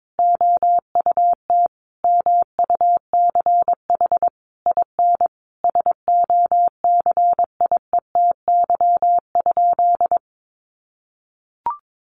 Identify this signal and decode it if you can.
Morse code, heard as a keyed tone, about 22 wpm, 700 Hz. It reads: OUT MUCH IN SOCIETY?
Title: Sentences from Top 700 Words 22wpm